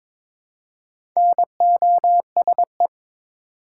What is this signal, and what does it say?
Morse code, heard as a keyed tone, about 23 wpm, 700 Hz. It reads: NOSE
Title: Four Letter Words - 22wpm